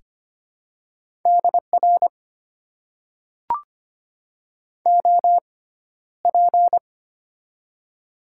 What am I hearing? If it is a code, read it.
Morse code, heard as a keyed tone, about 25 wpm, spaced as for 13 wpm, 700 Hz. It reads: DR OP
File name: Top 90 QSO Elements - 6x Character Spacing - 25wpm